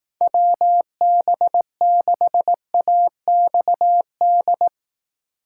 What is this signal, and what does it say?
Morse code, heard as a keyed tone, about 18 wpm, 700 Hz. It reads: WB6AXD
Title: Speed Racing - USA Call Signs 15wpm